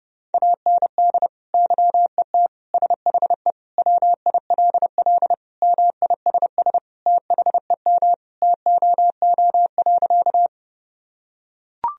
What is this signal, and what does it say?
Morse code, heard as a keyed tone, about 30 wpm, 700 Hz. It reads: AND YET SHE WILL MISS THEM TOO.
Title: Sentences from Top 1000 Words - Review Part 3 - 30wpm